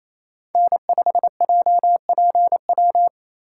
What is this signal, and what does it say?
Morse code, heard as a keyed tone, about 28 wpm, 700 Hz. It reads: N5JPW
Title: USA Call Signs - 5 Characters - 4x Character Spacing - 28wpm